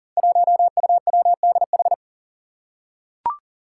Morse code, heard as a keyed tone, about 40 wpm, 700 Hz. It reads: 1UWDH
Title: Five Challenging Letter-Number Combinations 40wpm